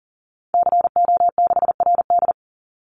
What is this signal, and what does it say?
Morse code, heard as a keyed tone, about 37 wpm, 700 Hz. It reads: CO6RD
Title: Call Signs Repeated 3x - 40wpm